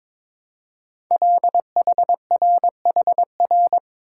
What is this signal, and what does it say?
Morse code, heard as a keyed tone, about 22 wpm, 700 Hz. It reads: LHRHR
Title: Code Groups - 5 Letter:Numbers - 22wpm